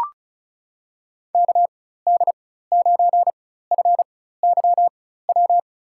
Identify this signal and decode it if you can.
Morse code, heard as a keyed tone, about 34 wpm, spaced as for 16 wpm, 700 Hz. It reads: KD9FYW
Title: USA Call Signs - 6 Characters - 4x Character Spacing - 35wpm